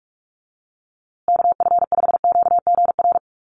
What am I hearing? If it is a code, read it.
Morse code, heard as a keyed tone, about 44 wpm, 700 Hz. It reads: KF5QGR
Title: USA Call Signs Repeated 3x 45wpm